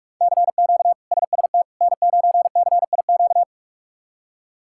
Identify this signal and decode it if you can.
Morse code, heard as a keyed tone, about 41 wpm, 700 Hz. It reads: CQ SST N9CIQ
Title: K1USN SST QSO Practice - 45wpm